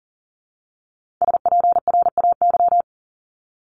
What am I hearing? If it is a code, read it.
Morse code, heard as a keyed tone, about 39 wpm, 700 Hz. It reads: SPRAY
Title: Multi-letter Phonemes Repeated 3x 40wpm